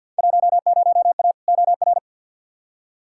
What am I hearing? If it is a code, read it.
Morse code, heard as a keyed tone, about 52 wpm, 700 Hz. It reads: 10A OR
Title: Field Day Exchanges - 50wpm